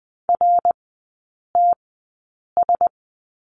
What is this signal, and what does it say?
Morse code, heard as a keyed tone, about 20 wpm, spaced as for 8 wpm, 690 Hz. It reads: RTS
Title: 060 Lesson 11e - Letter h - Three Characters 20wpm